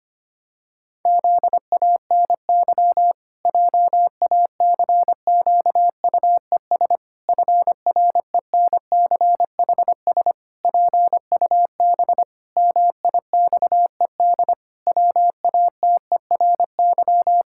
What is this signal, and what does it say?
Morse code, heard as a keyed tone, about 25 wpm, 700 Hz. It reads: ZANY JACQUES FRENCHS PUB MIXED WATERY
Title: Pangram Challenge v2 - 25wpm